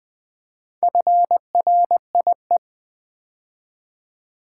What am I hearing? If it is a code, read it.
Morse code, heard as a keyed tone, about 20 wpm, 700 Hz. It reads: FRIE
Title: Build A Word - 2 Letters At A Time - 6-Letter Words - 20wpm